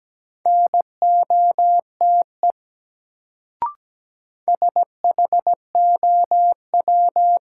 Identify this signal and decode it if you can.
Morse code, heard as a keyed tone, about 17 wpm, 700 Hz. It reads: NOTE SHOW
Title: Sets of 5 Words - New Word Drill - 17wpm